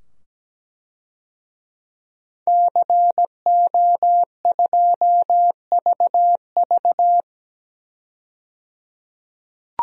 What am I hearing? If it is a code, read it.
Morse code, heard as a keyed tone, about 17 wpm, 700 Hz. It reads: CO2VV
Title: DXCC-100 Call Signs - 5 Characters - 17wpm